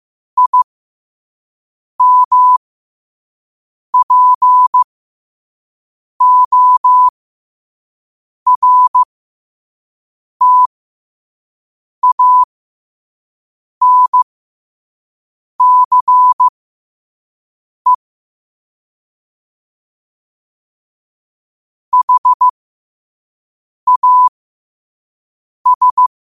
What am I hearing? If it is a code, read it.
Morse code, heard as a keyed tone, about 15 wpm, spaced as for 5 wpm, 1000 Hz. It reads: IMPORTANCE HAS
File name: qotdfiles / 5wpm